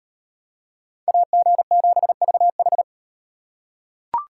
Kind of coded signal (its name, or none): Morse code